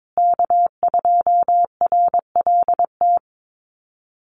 Morse code, heard as a keyed tone, about 22 wpm, 700 Hz. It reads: K2RLT
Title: USA Call Signs Repeated 2x 22wpm